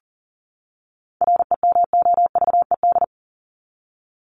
Morse code, heard as a keyed tone, about 40 wpm, 700 Hz. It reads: REMOVED
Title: ED Suffix - Top 100 Words with Base Words 40wpm